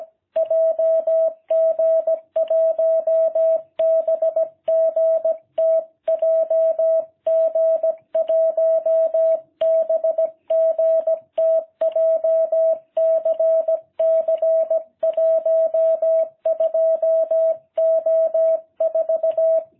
Morse code, heard as a keyed tone, about 17 wpm, 660 Hz. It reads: JG1BGT JG1BGT JCC 12O4